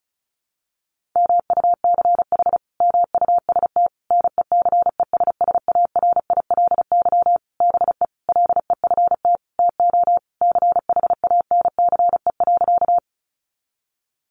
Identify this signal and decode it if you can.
Morse code, heard as a keyed tone, about 34 wpm, 700 Hz. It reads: MUCH MUST NECESSARILY BE LEFT TO CHANCE.
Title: Sentences from Top 1400 Words 35wpm